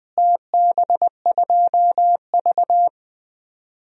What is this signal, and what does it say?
Morse code, heard as a keyed tone, about 20 wpm, 700 Hz. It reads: TB2V
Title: DXCC-100 Call Signs - 4 Characters - Repeated 3x - 20wpm